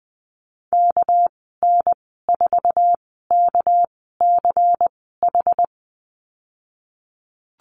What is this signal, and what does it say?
Morse code, heard as a keyed tone, about 20 wpm, spaced as for 15 wpm, 700 Hz. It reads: KN4KCH
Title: 213 Lesson 34g - Number 8 - Call Signs 20wpm